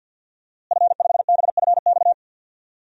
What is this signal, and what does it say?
Morse code, heard as a keyed tone, about 45 wpm, 700 Hz. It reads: F5BFX
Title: DXCC-100 Call Signs - 5 Characters - Repeated 3x - 50wpm